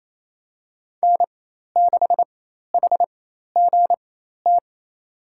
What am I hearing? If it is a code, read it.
Morse code, heard as a keyed tone, about 28 wpm, spaced as for 13 wpm, 700 Hz. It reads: N6HGT
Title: USA Call Signs - 4x Character Spacing - 28wpm